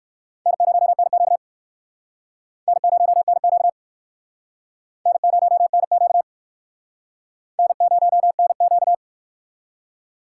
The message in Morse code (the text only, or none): N0NQ N0NQ N0NQ N0NQ